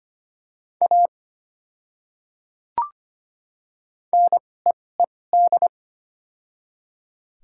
Morse code, heard as a keyed tone, about 25 wpm, spaced as for 18 wpm, 690 Hz. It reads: A NEED
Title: 2x Character Spacing - Top 500 Words - 25wpm